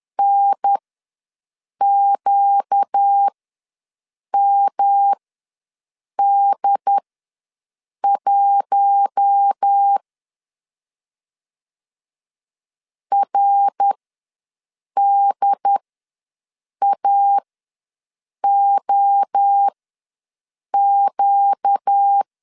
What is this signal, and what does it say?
Morse code, heard as a keyed tone, about 10 wpm, spaced as for 6 wpm, 800 Hz. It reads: NQMD1 RDAOQ
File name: LEZIONI / 5 LBVC / 5 LB ws 10wpm 3616